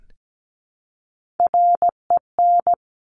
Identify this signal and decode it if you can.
Morse code, heard as a keyed tone, about 17 wpm, 700 Hz. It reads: REN
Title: Top Three Letter Combinations in Words 17wpm